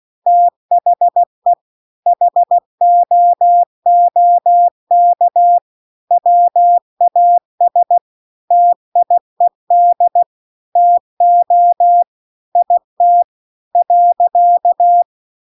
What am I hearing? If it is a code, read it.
Morse code, heard as a keyed tone, about 16 wpm, 700 Hz. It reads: THE HOOK WAS TIED TO IT .
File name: CW-The-Big-Twoheaded-River-2-16WPM